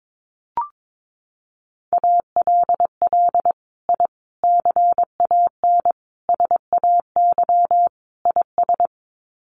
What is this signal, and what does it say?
Morse code, heard as a keyed tone, about 22 wpm, 700 Hz. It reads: ALL I CAN SAY IS
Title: Sets of 5 Words Encore2 22wpm